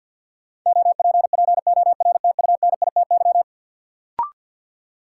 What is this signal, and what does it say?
Morse code, heard as a keyed tone, about 52 wpm, 700 Hz. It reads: OPPORTUNITY
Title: Sentences from Top 1000 Words - New Word Drill - 50wpm